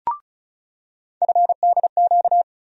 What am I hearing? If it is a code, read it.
Morse code, heard as a keyed tone, about 34 wpm, 700 Hz. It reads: FDQ
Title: USA Call Sign Postfixes - 3 Letters 35wpm